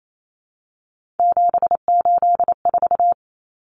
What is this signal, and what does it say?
Morse code, heard as a keyed tone, about 28 wpm, 700 Hz. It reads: 784
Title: Numbers - 3-Digit 28wpm